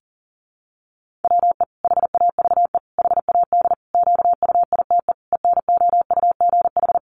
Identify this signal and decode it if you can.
Morse code, heard as a keyed tone, about 40 wpm, 700 Hz. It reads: WE HAVE HAD QUITE ENOUGH
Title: Sentences from Top 700 Words 40wpm